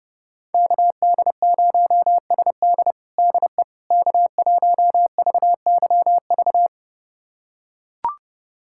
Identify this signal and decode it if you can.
Morse code, heard as a keyed tone, about 32 wpm, 700 Hz. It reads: KD0SD DE K1VYV
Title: USA Callsign DE Callsign 30wpm